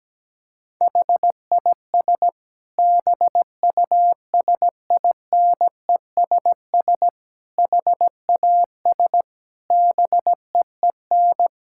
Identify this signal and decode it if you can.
Morse code, heard as a keyed tone, about 17 wpm, 700 Hz. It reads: HIS BUSINESS HAS BEEN
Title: Sentences from Top 1000 Words - Review Part 6 - 17wpm